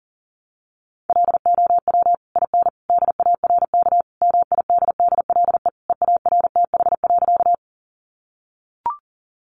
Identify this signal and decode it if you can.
Morse code, heard as a keyed tone, about 40 wpm, 700 Hz. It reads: LOW IN DARK MIDDLE EARTH.
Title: Sentences from Top 900 Words 40wpm